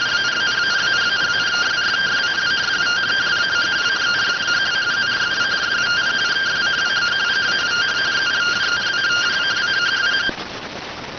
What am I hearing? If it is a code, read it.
Two-tone teleprinter signal, about 170 Hz shift.